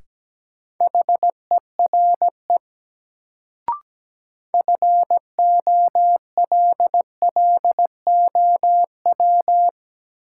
Morse code, heard as a keyed tone, about 17 wpm, 700 Hz. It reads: HERE FOLLOW
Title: Sentences from Top 200 Words - New Word Drill - 17wpm